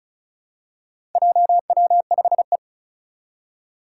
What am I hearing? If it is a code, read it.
Morse code, heard as a keyed tone, about 36 wpm, 700 Hz. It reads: JW5E
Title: Four Challenging Letter-Number Combinations 35wpm